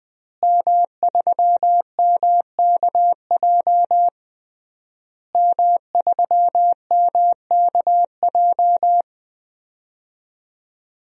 Morse code, heard as a keyed tone, about 20 wpm, 700 Hz. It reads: M3MKJ M3MKJ
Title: DXCC-100 Call Signs - 5 Characters - Repeated 3x - 20wpm